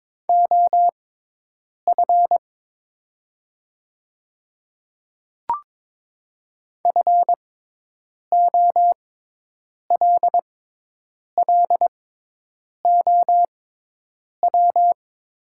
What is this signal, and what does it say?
Morse code, heard as a keyed tone, about 22 wpm, spaced as for 8 wpm, 700 Hz. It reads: OF FOLLOW
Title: 6x Character Spacing - Top 300 Words - 22wpm